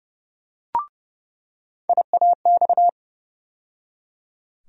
Morse code, heard as a keyed tone, about 30 wpm, 700 Hz. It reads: IAX
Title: USA Call Sign Postfixes - 3 Letters 30wpm